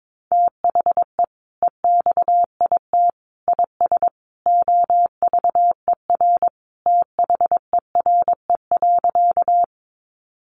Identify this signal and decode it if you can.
Morse code, heard as a keyed tone, about 22 wpm, 700 Hz. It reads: THE EXIT IS OVER THERE.
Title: Sentences from Top 1100 Words 22wpm